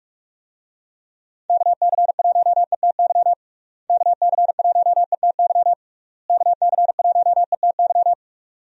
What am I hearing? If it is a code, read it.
Morse code, heard as a keyed tone, about 46 wpm, 700 Hz. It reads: KC1ETY KC1ETY KC1ETY
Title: USA Call Signs - 6 Characters - Repeated 3x - 45wpm